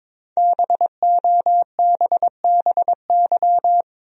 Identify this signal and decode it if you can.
Morse code, heard as a keyed tone, about 22 wpm, 700 Hz. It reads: BOBBY
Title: USA Common Names - 22wpm